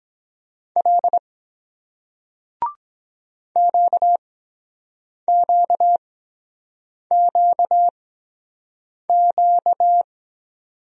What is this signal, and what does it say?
Morse code, heard as a keyed tone, about 23 wpm, spaced as for 7 wpm, 700 Hz. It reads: L QQQQ